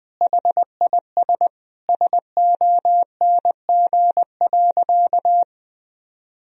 Morse code, heard as a keyed tone, about 21 wpm, 700 Hz. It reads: HIS SONG.
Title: Sentences from Top 1000 Words - Review Part 2 - 20wpm